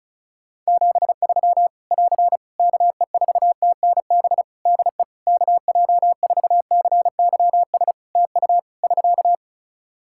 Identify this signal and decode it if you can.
Morse code, heard as a keyed tone, about 36 wpm, 700 Hz. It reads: 73 <AR> KE4TNB DE KJ4CYS TU <SK>
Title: Practice QSO - Exchange 4 - Ending - 35wpm